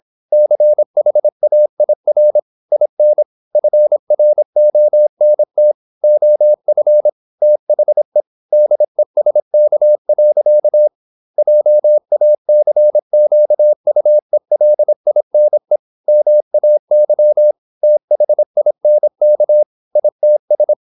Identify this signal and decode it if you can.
Morse code, heard as a keyed tone, about 27 wpm, 600 Hz. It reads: CHAIR IN FRONT OF THE DESK. JACQUELINE MAY THINK ITS
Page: Never 26wpm | CW med Gnister